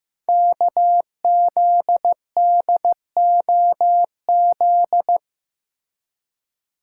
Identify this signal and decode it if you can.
Morse code, heard as a keyed tone, about 15 wpm, 700 Hz. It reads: KZDOZ
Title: Code Groups - 5 Letter:Numbers - 15wpm